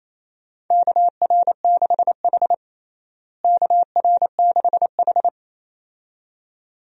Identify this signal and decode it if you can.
Morse code, heard as a keyed tone, about 28 wpm, 700 Hz. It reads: KR6H KR6H
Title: USA Call Signs - 4 Characters - Repeated 2x - 28wpm